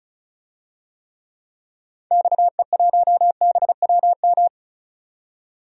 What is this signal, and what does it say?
Morse code, heard as a keyed tone, about 34 wpm, 700 Hz. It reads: XE1BWM
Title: DXCC-100 Call Signs - 6 Characters - 35wpm